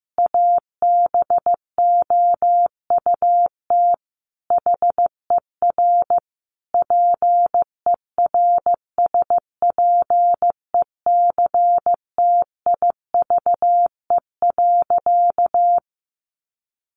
Morse code, heard as a keyed tone, about 15 wpm, 700 Hz. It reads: ABOUT HER PERSPECTIVE.
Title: Sentences from Top 1200 Words 15wpm